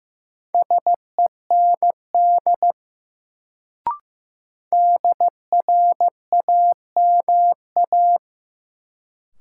Morse code, heard as a keyed tone, about 15 wpm, 700 Hz. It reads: SEND DRAMA
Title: Sentences from Top 1500 Words - New Word Drill Alt - 15wpm